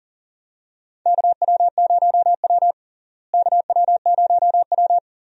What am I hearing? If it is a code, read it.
Morse code, heard as a keyed tone, about 40 wpm, 700 Hz. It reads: KW0W KW0W
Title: USA Call Signs - 4 Characters - Repeated 3x - 40wpm